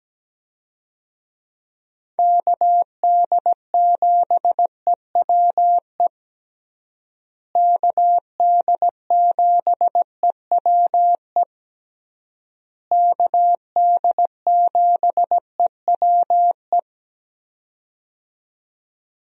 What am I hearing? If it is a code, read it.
Morse code, heard as a keyed tone, about 17 wpm, 700 Hz. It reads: KD7EWE KD7EWE KD7EWE
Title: USA Call Signs - 6 Characters - Repeated 3x - 17wpm